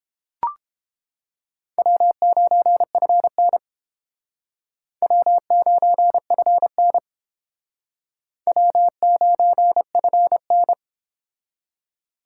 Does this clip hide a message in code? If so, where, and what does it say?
Yes: Morse code from 1.8 to 10.7 s, heard as a keyed tone, about 29 wpm, 700 Hz. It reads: W9FN W9FN W9FN